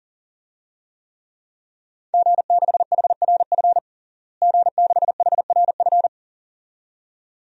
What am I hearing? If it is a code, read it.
Morse code, heard as a keyed tone, about 40 wpm, 700 Hz. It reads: G6HRF G6HRF
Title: DXCC-100 Call Signs - 5 Characters - Repeated 2x - 40wpm